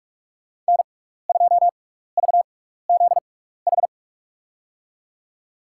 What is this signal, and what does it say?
Morse code, heard as a keyed tone, about 46 wpm, spaced as for 15 wpm, 700 Hz. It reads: N2VZH
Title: USA Call Signs - 6x Character Spacing - 45wpm